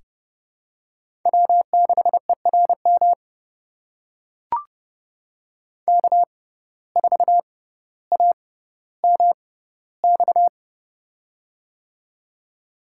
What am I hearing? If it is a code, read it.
Morse code, heard as a keyed tone, about 30 wpm, spaced as for 15 wpm, 700 Hz. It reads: W6ERM K4AMX
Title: USA Call Signs - 5 Characters - 6x Character Spacing - 30wpm